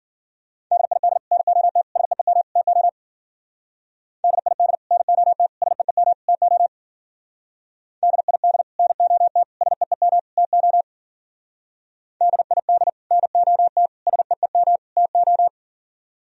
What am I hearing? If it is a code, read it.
Morse code, heard as a keyed tone, about 50 wpm, 700 Hz. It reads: DID NOT SEEM TO DID NOT SEEM TO DID NOT SEEM TO DID NOT SEEM TO